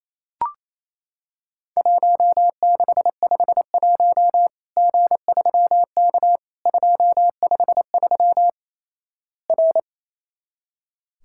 Morse code, heard as a keyed tone, about 28 wpm, 700 Hz. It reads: 1651 G3K 253 R
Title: Worked All Europe DX Contest - QTC Exchange - 28wpm